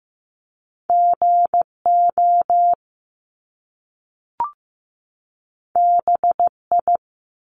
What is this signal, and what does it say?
Morse code, heard as a keyed tone, about 15 wpm, 700 Hz. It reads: GO BI
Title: Top Two Letter Combinations in Words 15wpm